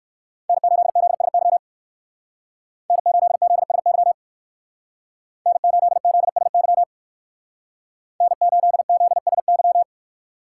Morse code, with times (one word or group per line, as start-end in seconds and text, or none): 0.5-1.6 N8ZSY
2.9-4.1 N8ZSY
5.5-6.8 N8ZSY
8.2-9.8 N8ZSY